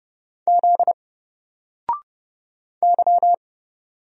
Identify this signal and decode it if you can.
Morse code, heard as a keyed tone, about 30 wpm, 700 Hz. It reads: Z Y
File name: Single-Letter-Number-30wpm